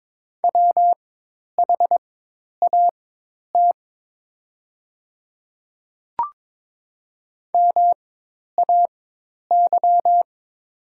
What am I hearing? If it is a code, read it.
Morse code, heard as a keyed tone, about 22 wpm, spaced as for 10 wpm, 700 Hz. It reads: WHAT MAY
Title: 4x Character Spacing - Top 200 Words - 22wpm